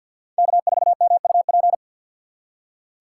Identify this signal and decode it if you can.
Morse code, heard as a keyed tone, about 48 wpm, 700 Hz. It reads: K4MUP
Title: USA Call Signs - 6x Character Spacing - 50wpm